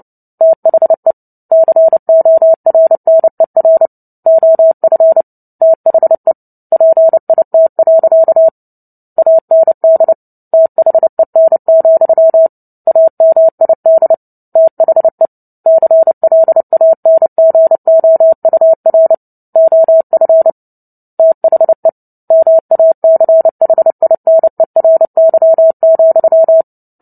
Morse code, heard as a keyed tone, about 30 wpm, 650 Hz. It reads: THE CORNER OF THE PIT. AND THEN, AMID THE CLANGOUR OF THE MACHINERY,
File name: Book Two / 03-Chapter 3 - 29 WPM